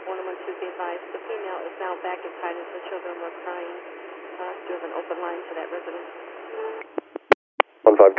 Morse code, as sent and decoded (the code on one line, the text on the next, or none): none